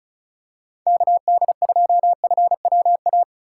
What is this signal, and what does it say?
Morse code, heard as a keyed tone, about 36 wpm, 700 Hz. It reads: KD2FWA
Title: USA Call Signs - 4x Character Spacing - 35wpm